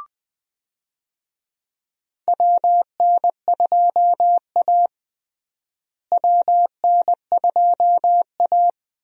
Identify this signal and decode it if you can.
Morse code, heard as a keyed tone, about 20 wpm, 700 Hz. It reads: WN2A WN2A
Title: USA Call Signs - 4 Characters - Repeated 2x - 20wpm